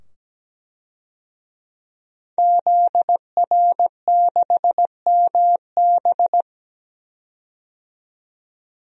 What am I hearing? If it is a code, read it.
Morse code, heard as a keyed tone, about 17 wpm, 700 Hz. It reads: ZR6MB
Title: DXCC-100 Call Signs - 5 Characters - 17wpm